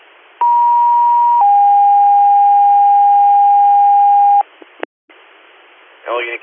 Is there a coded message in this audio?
Yes: two-tone page from 0.4 to 4.4 s, heard as a one-second tone, then a three-second tone, about 950 then 810 Hz, with speech.